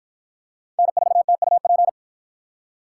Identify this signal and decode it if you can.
Morse code, heard as a keyed tone, about 53 wpm, 700 Hz. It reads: N4TUP